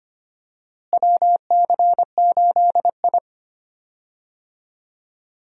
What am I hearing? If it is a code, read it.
Morse code, heard as a keyed tone, about 25 wpm, 700 Hz. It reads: WC8I
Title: USA Call Signs - 4 Characters - Repeated 2x - 25wpm